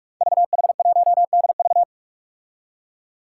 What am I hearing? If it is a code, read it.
Morse code, heard as a keyed tone, about 44 wpm, 700 Hz. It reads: VH1DV
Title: Five Challenging Letter-Number Combinations 45wpm